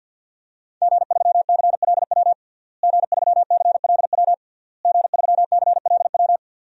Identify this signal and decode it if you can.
Morse code, heard as a keyed tone, about 48 wpm, 700 Hz. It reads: G3CLW G3CLW G3CLW
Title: DXCC-100 Call Signs - 5 Characters - Repeated 3x - 50wpm